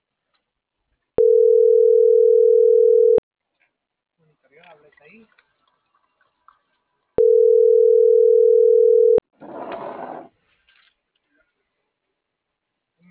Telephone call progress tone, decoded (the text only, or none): ringback tone